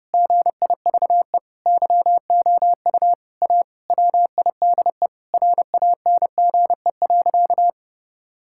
Morse code, heard as a keyed tone, about 30 wpm, 700 Hz. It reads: GIVE YOU A WIDE RANGE.
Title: Sentences from Top 1000 Words - Review Part 6 - 30wpm